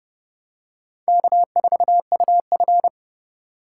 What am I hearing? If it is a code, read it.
Morse code, heard as a keyed tone, about 28 wpm, 700 Hz. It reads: K4UF